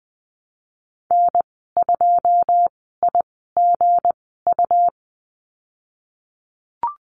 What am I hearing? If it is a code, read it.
Morse code, heard as a keyed tone, about 20 wpm, spaced as for 15 wpm, 700 Hz. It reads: N2IGU
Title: 161 Lesson 26g - Letter g - Call Signs 20wpm